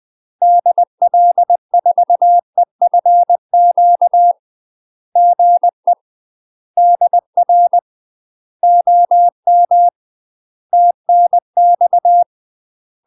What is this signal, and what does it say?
Morse code, heard as a keyed tone, about 20 wpm, 700 Hz. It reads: DL4EFQ GE DR OM TNX
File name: UK MASTER QSOs TRAINING/20WPM